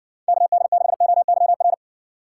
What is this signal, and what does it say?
Morse code, heard as a keyed tone, about 60 wpm, 700 Hz. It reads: KD6OYR